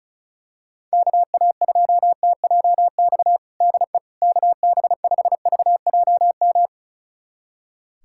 Morse code, heard as a keyed tone, about 36 wpm, 700 Hz. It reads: KA2TJX DE KB5VJM
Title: USA Callsign DE Callsign 35wpm